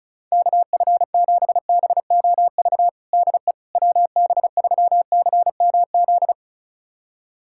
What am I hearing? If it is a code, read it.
Morse code, heard as a keyed tone, about 36 wpm, 700 Hz. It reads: KF7BOV DE WB3CMZ
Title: USA Callsign DE Callsign 35wpm